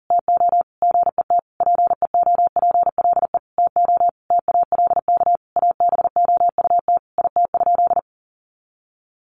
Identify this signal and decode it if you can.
Morse code, heard as a keyed tone, about 40 wpm, 700 Hz. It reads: TO GET PEOPLE TO TALK ABOUT IT?
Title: Sentences from Top 1400 Words 40wpm